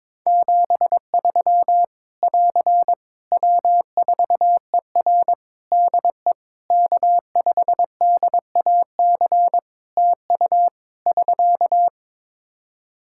Morse code, heard as a keyed tone, about 21 wpm, 700 Hz. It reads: 73 <AR> W4ER DE K5DAC TU <SK>
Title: K1USN SST QSO Practice 22wpm